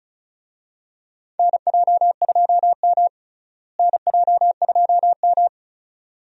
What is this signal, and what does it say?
Morse code, heard as a keyed tone, about 34 wpm, 700 Hz. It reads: NJ2M NJ2M
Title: USA Call Signs - 4 Characters - Repeated 2x - 35wpm